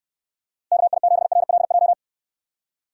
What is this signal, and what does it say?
Morse code, heard as a keyed tone, about 69 wpm, 700 Hz. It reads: KI7RLJ